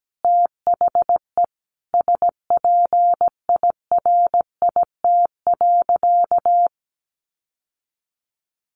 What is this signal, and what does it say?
Morse code, heard as a keyed tone, about 17 wpm, 700 Hz. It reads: THE SPIRIT.
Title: Sentences from Top 1400 Words 17wpm